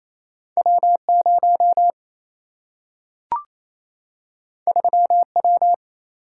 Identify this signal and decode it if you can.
Morse code, heard as a keyed tone, about 28 wpm, 700 Hz. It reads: W0 3W
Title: Single-Letter Single-Number 28wpm